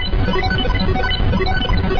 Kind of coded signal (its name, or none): none